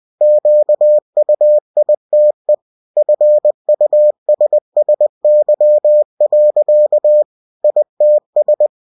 Morse code, heard as a keyed tone, about 20 wpm, 600 Hz. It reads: QUITE FUSSY. ITS
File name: cw-books / winnie-the-pooh / winnie-20wpm-page-0105